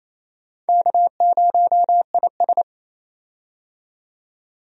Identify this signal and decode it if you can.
Morse code, heard as a keyed tone, about 28 wpm, 700 Hz. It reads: K0IS
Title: USA Call Signs - 4 Characters - 28wpm